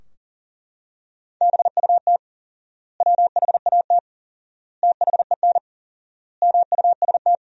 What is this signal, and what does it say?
Morse code, heard as a keyed tone, about 40 wpm, 700 Hz. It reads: BUT WHAT THEN MUST
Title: 4x Word-Spacing - Sentences from Top 500 Words - 40wpm